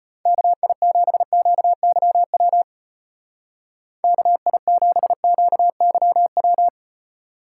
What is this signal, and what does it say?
Morse code, heard as a keyed tone, about 32 wpm, 700 Hz. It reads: KI7QYW KI7QYW